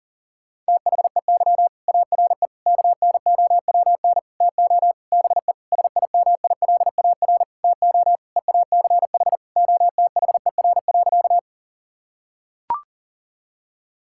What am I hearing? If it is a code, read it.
Morse code, heard as a keyed tone, about 40 wpm, 700 Hz. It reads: THEY ARE KNOWN TO BE SIMILAR TO EACH OTHER.
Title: Sentences from Top 1000 Words - Review Part 5 - 40wpm